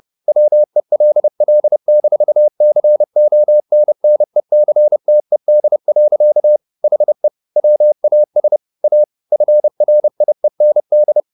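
Morse code, heard as a keyed tone, about 30 wpm, 600 Hz. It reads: WELL-CONNECTED. HE WAS A FRIEND
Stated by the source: Morse code at 30 wpm, the novel Never by Ken Follett in English, from CW med Gnister.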